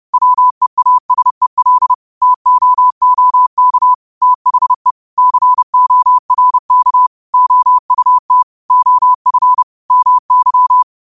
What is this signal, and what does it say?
Morse code, heard as a keyed tone, about 30 wpm, 1000 Hz. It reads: WEASEL TOOK THE CORK OUT OF MY
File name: qotdfiles / 30wpm